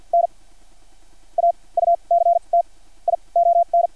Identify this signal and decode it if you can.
Morse code, heard as a keyed tone, about 50 wpm, spaced as for 27 wpm, 690 Hz. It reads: N AUCTION
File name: archived / CRSnet